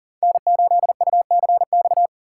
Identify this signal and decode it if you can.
Morse code, heard as a keyed tone, about 40 wpm, 700 Hz. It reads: N8UCX
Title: USA Call Signs - 5 Characters - 6x Character Spacing - 40wpm